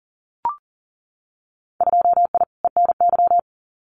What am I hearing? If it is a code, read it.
Morse code, heard as a keyed tone, about 42 wpm, 700 Hz. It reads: 2I ENY
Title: Winter Field Day Exchanges Encore 40wpm